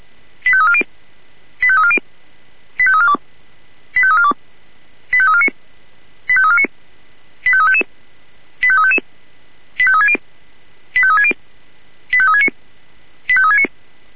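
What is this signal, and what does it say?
Five-tone selective call (ZVEI), heard as quick runs of five tones, tones of about 70 ms. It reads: 85340 85340 85342 85342 85349 85349 85355 85355 85360 85360 85369 85369
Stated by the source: ZVEI five-tone selective call received on SDR.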